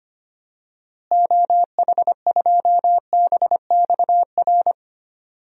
Morse code, heard as a keyed tone, about 25 wpm, 700 Hz. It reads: OH2BXR
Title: DXCC-100 Call Signs - 6 Characters - Repeated 2x - 25wpm